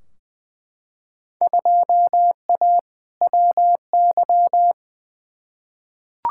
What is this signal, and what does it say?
Morse code, heard as a keyed tone, about 20 wpm, 700 Hz. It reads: 2A WY
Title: Field Day Exchanges - 20wpm